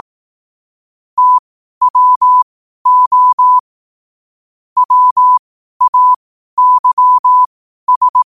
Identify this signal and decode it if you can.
Morse code, heard as a keyed tone, about 18 wpm, spaced as for 12 wpm, 1000 Hz. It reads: TWO WAYS
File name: qotdfiles / 12wpm